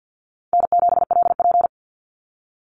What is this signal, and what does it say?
Morse code, heard as a keyed tone, about 48 wpm, 700 Hz. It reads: N7LP
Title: NA SOTA Activator Callsigns 50wpm